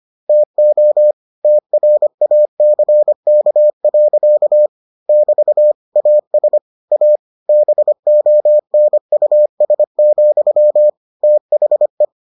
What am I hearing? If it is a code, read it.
Morse code, heard as a keyed tone, about 25 wpm, 600 Hz. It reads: TO TRACK. <BT> AS A BONUS, THE